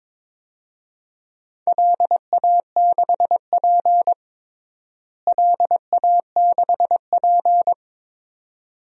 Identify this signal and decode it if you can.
Morse code, heard as a keyed tone, about 22 wpm, 700 Hz. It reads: LA6P LA6P
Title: DXCC-100 Call Signs - 4 Characters - Repeated 2x - 22wpm